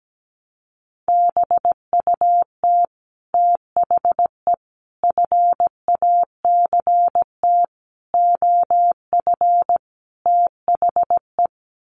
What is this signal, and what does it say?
Morse code, heard as a keyed tone, about 17 wpm, 700 Hz. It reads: BUT THE FACT OF THE
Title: Sets of 5 Words Encore2 17wpm